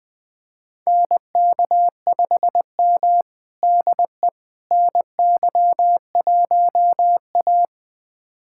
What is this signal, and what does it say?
Morse code, heard as a keyed tone, about 20 wpm, 700 Hz. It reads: NK5M DE NY1A
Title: USA Callsign DE Callsign - 4 Characters - 20wpm